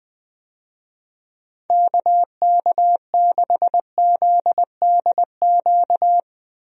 Morse code, heard as a keyed tone, about 21 wpm, 700 Hz. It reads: KK6ZDQ
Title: USA Call Signs - 6 Characters - 20wpm